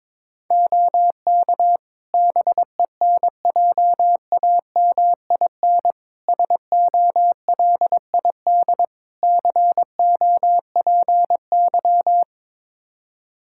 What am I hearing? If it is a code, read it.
Morse code, heard as a keyed tone, about 22 wpm, 700 Hz. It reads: OK BENJAMIN SOLID COPY
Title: Practice QSO - Exchange 3 - More Info - 22wpm